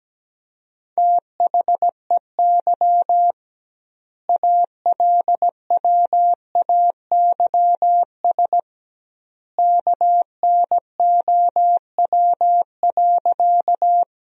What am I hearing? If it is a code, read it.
Morse code, heard as a keyed tone, about 17 wpm, 700 Hz. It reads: THEY ALWAYS KNOW.
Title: 2x Word-Spacing - Sentences from Top 300 Words - 17wpm